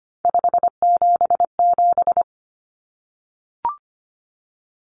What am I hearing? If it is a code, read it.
Morse code, heard as a keyed tone, about 25 wpm, 700 Hz. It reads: 577
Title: Common Cut Numbers - 3-Digit 25wpm